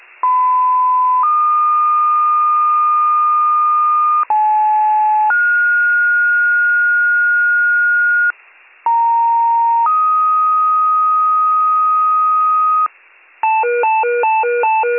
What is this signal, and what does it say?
Two-tone page, heard as a one-second tone, then a three-second tone, about 1010 then 1220 Hz, about 860 then 1430 Hz, about 930 then 1250 Hz, with noise. It is heard from 0.2 to 12.9 s.